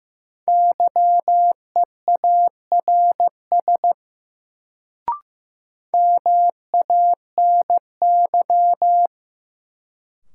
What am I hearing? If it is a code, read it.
Morse code, heard as a keyed tone, about 15 wpm, 700 Hz. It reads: YEARS MANY
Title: Sets of 2 Words - New Word Drill - 15wpm